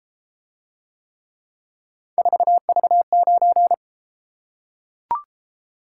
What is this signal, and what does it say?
Morse code, heard as a keyed tone, about 32 wpm, 700 Hz. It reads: 4V9